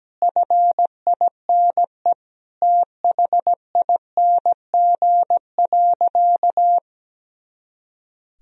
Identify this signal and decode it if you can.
Morse code, heard as a keyed tone, about 17 wpm, 700 Hz. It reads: FINE THING.
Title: Sentences from Top 500 Words and Abbreviations 17wpm